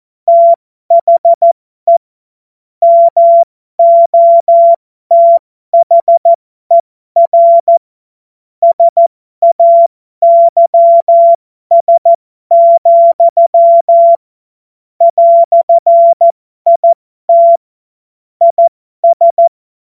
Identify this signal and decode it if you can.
Morse code, heard as a keyed tone, about 14 wpm, spaced as for 12 wpm, 680 Hz. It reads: THE MOTHER SAYS, "IT IS
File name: Morsekurs / Witze